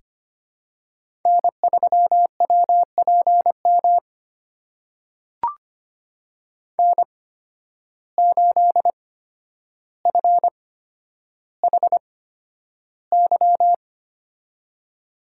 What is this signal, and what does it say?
Morse code, heard as a keyed tone, about 25 wpm, spaced as for 11 wpm, 700 Hz. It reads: N3WPM N8FHY
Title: USA Call Signs - 5 Characters - 8x Character Spacing - 25wpm